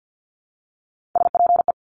Morse code, heard as a keyed tone, about 50 wpm, 700 Hz. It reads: SPE
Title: Top Three Letter Combinations in Words 50wpm